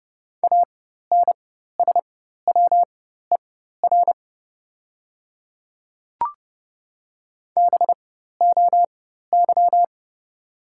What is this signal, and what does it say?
Morse code, heard as a keyed tone, about 30 wpm, spaced as for 14 wpm, 700 Hz. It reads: ANSWER BOY
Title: 4x Character Spacing - Top 500 Words - 30wpm